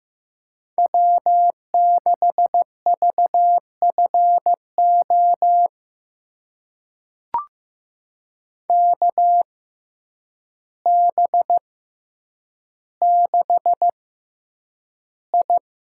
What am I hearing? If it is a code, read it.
Morse code, heard as a keyed tone, about 15 wpm, spaced as for 8 wpm, 700 Hz. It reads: W6VFO KB6I
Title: USA Call Signs - 6x Character Spacing - 15wpm